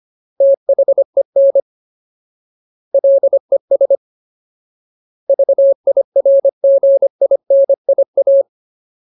Morse code, heard as a keyed tone, about 25 wpm, 550 Hz. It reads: THEN LES VIRGINIA
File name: Ultimate Word List / 25WPM Ultimate